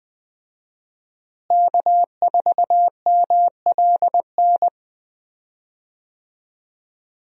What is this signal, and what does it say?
Morse code, heard as a keyed tone, about 20 wpm, 700 Hz. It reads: K4MLN
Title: USA Call Signs - 5 Characters - Repeated 3x - 20wpm